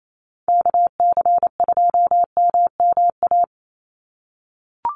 Morse code, heard as a keyed tone, about 28 wpm, 700 Hz. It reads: KC2MMA
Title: USA Call Signs 28wpm